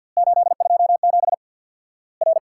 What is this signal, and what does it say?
Morse code, heard as a keyed tone, about 50 wpm, 700 Hz. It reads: 827 R
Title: Worked All Europe DX Contest - QTC Exchange with Repeats - 50wpm